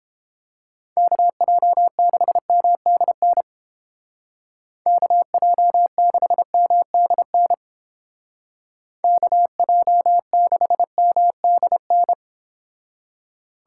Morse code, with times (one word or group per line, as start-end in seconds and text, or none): 1.0-3.4 KJ6MDN
4.9-7.5 KJ6MDN
9.0-12.1 KJ6MDN